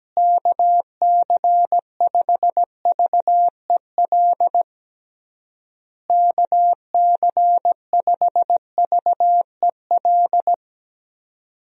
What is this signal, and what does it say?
Morse code, heard as a keyed tone, about 17 wpm, 700 Hz. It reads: KC5VEL KC5VEL
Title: USA Call Signs - 6 Characters - Repeated 3x - 17wpm